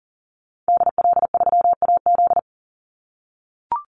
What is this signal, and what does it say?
Morse code, heard as a keyed tone, about 40 wpm, 700 Hz. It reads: DL3AZ
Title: Call Signs - 40wpm